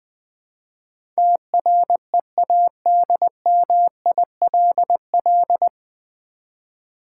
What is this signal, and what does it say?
Morse code, heard as a keyed tone, about 20 wpm, 700 Hz. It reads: TREADMILL
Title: Easy Words with Hints - 20wpm